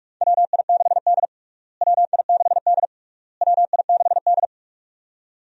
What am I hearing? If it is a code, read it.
Morse code, heard as a keyed tone, about 43 wpm, 700 Hz. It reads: WI6D WI6D WI6D
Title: USA Call Signs - 4 Characters - Repeated 3x - 45wpm